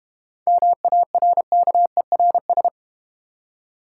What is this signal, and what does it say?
Morse code, heard as a keyed tone, about 32 wpm, 690 Hz. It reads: MARKERS